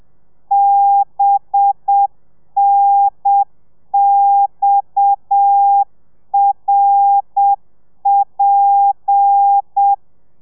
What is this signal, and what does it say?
Morse code, heard as a keyed tone, about 7 wpm, 800 Hz. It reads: BNXRP
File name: telegrafia / i2tfj / Lez7 30